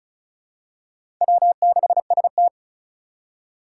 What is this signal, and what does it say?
Morse code, heard as a keyed tone, about 36 wpm, 700 Hz. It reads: W6ST
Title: USA Call Signs - 2x Character Spacing - 35wpm